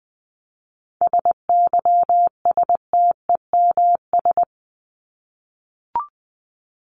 Morse code, heard as a keyed tone, about 20 wpm, 700 Hz. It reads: SYSTEMS
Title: ES:S Suffix - Top 100 Words with Base Words 20wpm